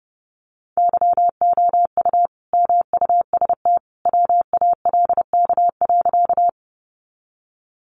Morse code, heard as a keyed tone, about 30 wpm, 700 Hz. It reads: YOU MUST WALK.
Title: Sentences from Top 300 Words 30wpm